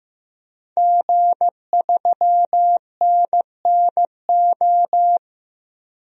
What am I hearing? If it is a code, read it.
Morse code, heard as a keyed tone, about 15 wpm, 700 Hz. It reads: G3NNO
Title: DXCC-100 Call Signs - 5 Characters - 15wpm